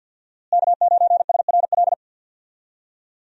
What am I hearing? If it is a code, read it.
Morse code, heard as a keyed tone, about 51 wpm, 700 Hz. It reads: K9SRL
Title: USA Call Signs - 5 Characters - Repeated 2x - 50wpm